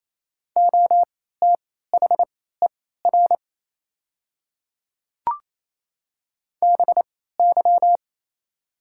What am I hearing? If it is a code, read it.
Morse code, heard as a keyed tone, about 28 wpm, spaced as for 16 wpm, 690 Hz. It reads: OTHER BY
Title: 3x Character Spacing - Top 100 Words - 28wpm